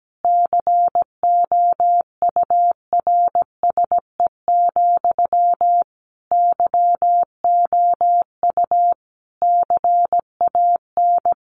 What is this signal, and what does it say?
Morse code, heard as a keyed tone, about 17 wpm, 700 Hz. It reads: COURSE, YOU CAN
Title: Sentences from Top 1500 Words 17wpm